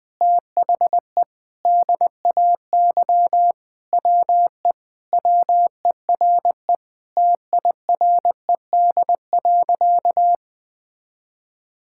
Morse code, heard as a keyed tone, about 20 wpm, 700 Hz. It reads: THE DAY WE WERE TIRED.
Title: Sentences with 5-Word Phrases - 20wpm